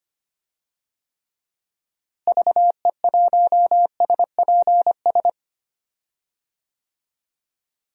Morse code, heard as a keyed tone, about 25 wpm, 700 Hz. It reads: VE1SPS
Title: DXCC-100 Call Signs - 6 Characters - Repeated 2x - 25wpm